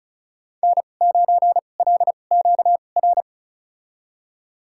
Morse code, heard as a keyed tone, about 36 wpm, spaced as for 25 wpm, 700 Hz. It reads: N9LQR
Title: USA Call Signs - 5 Characters - 2x Character Spacing - 35wpm